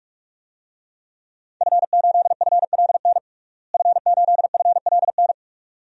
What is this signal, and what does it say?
Morse code, heard as a keyed tone, about 44 wpm, 700 Hz. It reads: F8FLN F8FLN
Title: DXCC-100 Call Signs - 5 Characters - Repeated 2x - 45wpm